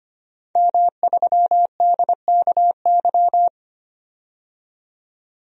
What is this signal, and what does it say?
Morse code, heard as a keyed tone, about 25 wpm, 700 Hz. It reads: M3DKY
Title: DXCC-100 Call Signs - 5 Characters - 25wpm